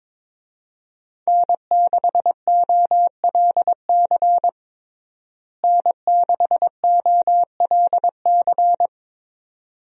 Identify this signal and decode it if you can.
Morse code, heard as a keyed tone, about 22 wpm, 700 Hz. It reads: N6OLC N6OLC
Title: USA Call Signs - 5 Characters - Repeated 2x - 22wpm